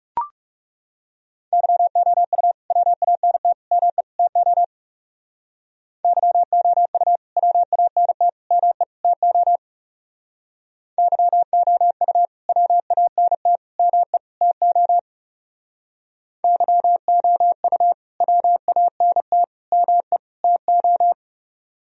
Morse code, heard as a keyed tone, about 36 wpm, 700 Hz. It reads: YOU WANT ME TO YOU WANT ME TO YOU WANT ME TO YOU WANT ME TO
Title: Speed-Racing - Sets of 4 Words 30wpm